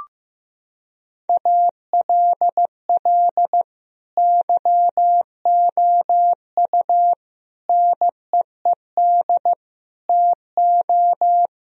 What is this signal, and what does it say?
Morse code, heard as a keyed tone, about 15 wpm, 700 Hz. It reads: ALL YOU NEED TO
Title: Sentences from Top 1000 Words - Review Part 4 - 15wpm